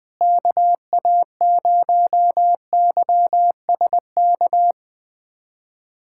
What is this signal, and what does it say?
Morse code, heard as a keyed tone, about 21 wpm, 700 Hz. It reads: KA0YSK
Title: USA Callsign DE Callsign 20wpm